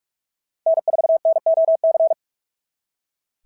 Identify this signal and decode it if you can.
Morse code, heard as a keyed tone, about 41 wpm, 690 Hz. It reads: N4NOC
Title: SOTA QSO Practice 45wpm